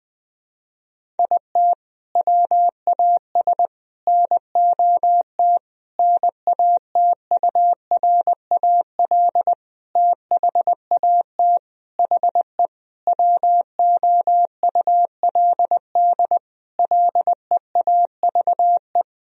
Morse code, heard as a keyed tone, about 20 wpm, 700 Hz. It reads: IT WAS NOT NATURAL THAT HE WOULD LEAVE
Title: Sentences from Top 1000 Words - Review Part 4 - 20wpm